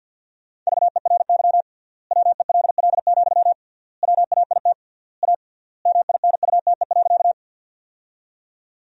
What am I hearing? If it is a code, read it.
Morse code, heard as a keyed tone, about 48 wpm, 700 Hz. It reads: VERY WELL, WAIT A MINUTE.
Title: 3x Word-Spacing - Sentences from Top 500 Words - 50wpm